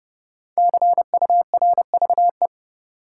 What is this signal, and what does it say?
Morse code, heard as a keyed tone, about 29 wpm, 700 Hz. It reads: CURVE
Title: Five Letter Words - 30wpm